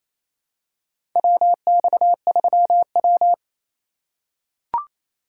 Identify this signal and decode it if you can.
Morse code, heard as a keyed tone, about 28 wpm, 700 Hz. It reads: WX3W
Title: USA Call Signs - 4 Characters - 3x Character Spacing - 28wpm